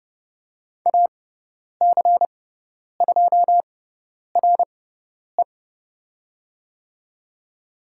Morse code, heard as a keyed tone, about 30 wpm, spaced as for 10 wpm, 690 Hz. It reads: AC2RE
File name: Lesson 17g - Letter u - Call Signs 30wpm